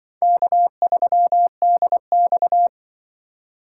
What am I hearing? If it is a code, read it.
Morse code, heard as a keyed tone, about 24 wpm, 700 Hz. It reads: K3DX
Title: Speed Racing - USA 1x2 Call Signs 20wpm